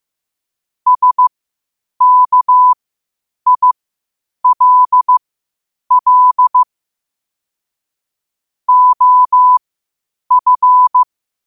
Morse code, heard as a keyed tone, about 15 wpm, spaced as for 8 wpm, 1000 Hz. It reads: SKILL OF